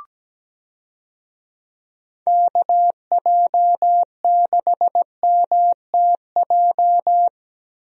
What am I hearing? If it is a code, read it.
Morse code, heard as a keyed tone, about 17 wpm, 700 Hz. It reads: KJ6MTJ
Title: USA Call Signs - 6 Characters - Repeated 3x - 17wpm